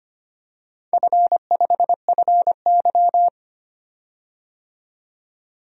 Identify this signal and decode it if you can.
Morse code, heard as a keyed tone, about 25 wpm, 700 Hz. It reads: F5FY
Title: DXCC-100 Call Signs - 4 Characters - 25wpm